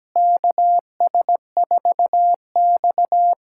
Morse code, heard as a keyed tone, about 17 wpm, 700 Hz. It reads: KS4X
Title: USA Callsign DE Callsign - 4 Characters - 17wpm